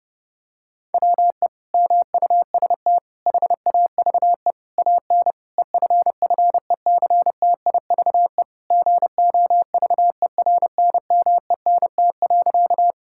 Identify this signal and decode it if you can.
Morse code, heard as a keyed tone, about 30 wpm, 700 Hz. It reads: WE MUST HAVE AN EFFECTIVE GOVERNMENT.
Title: Sentences from Top 1000 Words - Review Part 6 - 30wpm